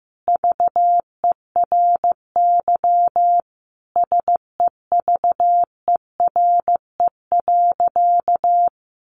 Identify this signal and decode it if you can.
Morse code, heard as a keyed tone, about 15 wpm, 700 Hz. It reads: VERY SEVERE.
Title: Sentences from Top 1400 Words 15wpm